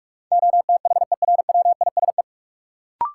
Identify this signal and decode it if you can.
Morse code, heard as a keyed tone, about 46 wpm, 700 Hz. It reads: OTHERWISE
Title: Sentences from Top 1200 Words - New Word Drill - 45wpm